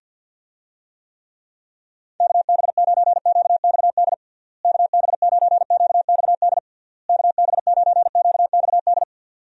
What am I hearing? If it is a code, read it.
Morse code, heard as a keyed tone, about 46 wpm, 700 Hz. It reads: KB9QXD KB9QXD KB9QXD
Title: USA Call Signs - 6 Characters - Repeated 3x - 50wpm